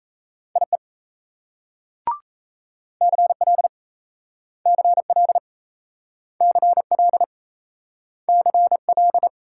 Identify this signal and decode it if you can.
Morse code, heard as a keyed tone, about 36 wpm, 700 Hz. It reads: IE CL CL CL CL